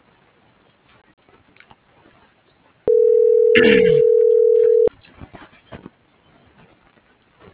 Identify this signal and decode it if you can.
Ringback tone (North America), 440 + 480 Hz.